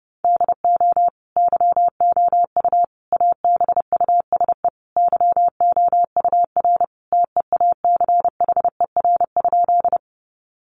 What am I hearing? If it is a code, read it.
Morse code, heard as a keyed tone, about 30 wpm, 700 Hz. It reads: DO YOU ABUSE YOUR TEACHER?
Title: Sentences from Top 1300 Words 30wpm